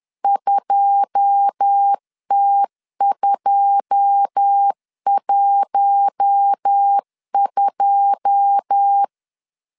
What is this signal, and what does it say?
Morse code, heard as a keyed tone, about 10 wpm, 800 Hz. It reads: 2T212
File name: LEZIONI / 9 2 10wpm 2736